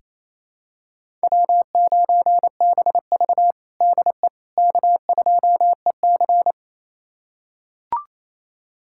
Morse code, heard as a keyed tone, about 28 wpm, 700 Hz. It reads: W9BV DE K2EC
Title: USA Callsign DE Callsign - 4 Characters - 28wpm